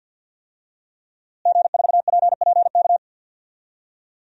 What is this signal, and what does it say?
Morse code, heard as a keyed tone, about 52 wpm, 700 Hz. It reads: G4PPK
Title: DXCC-100 Call Signs - 5 Characters - Repeated 2x - 50wpm